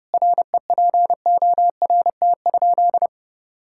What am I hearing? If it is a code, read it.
Morse code, heard as a keyed tone, about 32 wpm, 700 Hz. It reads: REPORT?
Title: Sentences from Top 1000 Words - Review Part 6 - 30wpm